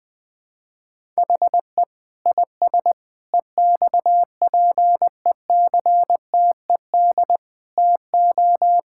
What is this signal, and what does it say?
Morse code, heard as a keyed tone, about 20 wpm, 700 Hz. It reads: HE IS EXPECTED TO
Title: Sentences with 7-Word Phrases - 20wpm